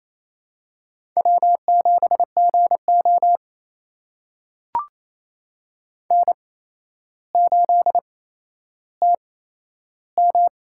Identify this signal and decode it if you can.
Morse code, heard as a keyed tone, about 28 wpm, spaced as for 12 wpm, 700 Hz. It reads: W7GO N8TM
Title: USA Call Signs - 4 Characters - 8x Character Spacing - 28wpm